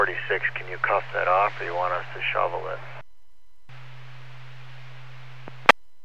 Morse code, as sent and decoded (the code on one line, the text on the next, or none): none